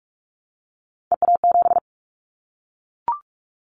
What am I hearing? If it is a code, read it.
Morse code, heard as a keyed tone, about 44 wpm, 700 Hz. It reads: EA7
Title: Callsign Prefixes for Top 25 DXCC - 45wpm